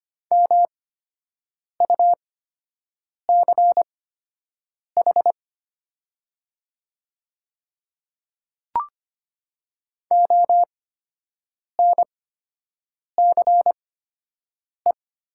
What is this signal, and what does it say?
Morse code, heard as a keyed tone, about 25 wpm, spaced as for 7 wpm, 700 Hz. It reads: MUCH ONCE
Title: 8x Character Spacing - Top 300 Words - 25wpm